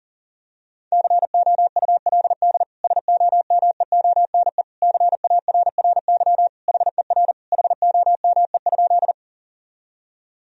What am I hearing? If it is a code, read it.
Morse code, heard as a keyed tone, about 40 wpm, 700 Hz. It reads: COULD SOMEONE CARRY HER HOME?
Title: Sentences from Top 1000 Words - Review Part 4 - 40wpm